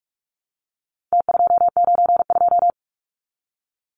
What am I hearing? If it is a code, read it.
Morse code, heard as a keyed tone, about 44 wpm, 700 Hz. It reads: T292
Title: Numbers - Contest Style - 4-Digit 45wpm